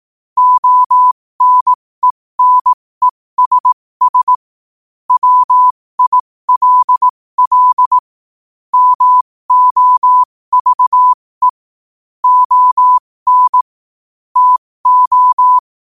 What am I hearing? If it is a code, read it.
Morse code, heard as a keyed tone, about 18 wpm, spaced as for 15 wpm, 1000 Hz. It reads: ONENESS WILL MOVE ON TO
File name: qotdfiles / 15wpm